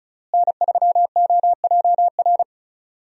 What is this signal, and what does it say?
Morse code, heard as a keyed tone, about 34 wpm, 700 Hz. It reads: N3OJR
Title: USA Call Signs - 5 Characters - Repeated 2x - 35wpm